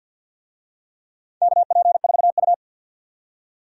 Morse code, heard as a keyed tone, about 48 wpm, 700 Hz. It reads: KP4U
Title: DXCC-100 Call Signs - 4 Characters - 50wpm